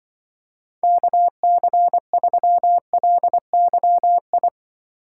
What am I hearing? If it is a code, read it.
Morse code, heard as a keyed tone, about 24 wpm, 700 Hz. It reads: KC3LYI
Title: Speed Racing - USA 2x3 Call Signs 20wpm